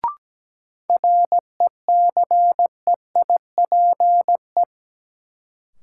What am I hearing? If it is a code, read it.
Morse code, heard as a keyed tone, about 17 wpm, 700 Hz. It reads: RECEIPE
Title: Sentences from Top 1200 Words - New Word Drill - 17wpm